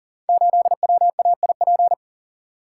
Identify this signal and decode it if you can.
Morse code, heard as a keyed tone, about 39 wpm, 700 Hz. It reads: 8WAIP
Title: Code Groups - 5 Letter:Numbers - 40wpm